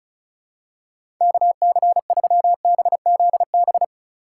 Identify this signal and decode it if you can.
Morse code, heard as a keyed tone, about 36 wpm, 700 Hz. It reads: KC3BZB
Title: USA Call Signs - 6 Characters - 4x Character Spacing - 35wpm